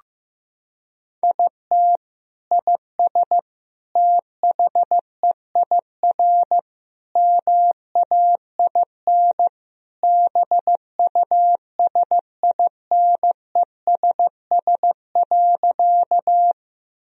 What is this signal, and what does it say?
Morse code, heard as a keyed tone, about 15 wpm, 700 Hz. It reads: IT IS THEIR MAIN BUSINESS.
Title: Sentences from Top 1000 Words - Review Part 2 - 15wpm